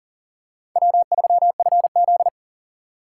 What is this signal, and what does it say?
Morse code, heard as a keyed tone, about 40 wpm, 690 Hz. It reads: W3FZ
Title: USA Call Signs - 3x Character Spacing - 40wpm